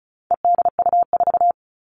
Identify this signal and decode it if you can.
Morse code, heard as a keyed tone, about 34 wpm, 700 Hz. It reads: EDU4
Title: All Cut Numbers - 4-Digit 35wpm